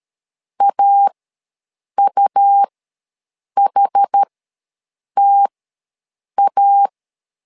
Morse code, heard as a keyed tone, about 13 wpm, spaced as for 7 wpm, 800 Hz. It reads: AUHTA
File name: LEZIONI / 2 UHDN / 2 UH ws 12wpm 1730